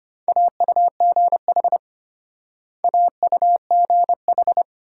Morse code, heard as a keyed tone, about 30 wpm, 700 Hz. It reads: AUGH AUGH
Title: Speed-Racing Multi-letter Phonemes - 25wpm